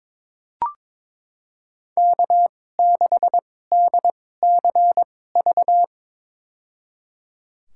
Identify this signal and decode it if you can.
Morse code, heard as a keyed tone, about 22 wpm, spaced as for 16 wpm, 700 Hz. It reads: K6DCV
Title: USA Call Signs - 5 Characters - 2x Character Spacing - 22wpm